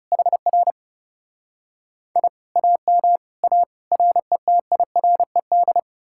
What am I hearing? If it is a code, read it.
Morse code, heard as a keyed tone, about 38 wpm, 700 Hz. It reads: HR I AM A RETIRED